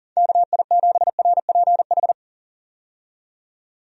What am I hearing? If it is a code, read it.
Morse code, heard as a keyed tone, about 40 wpm, 700 Hz. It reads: KI7RPH
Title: USA Call Signs - 6 Characters - 40wpm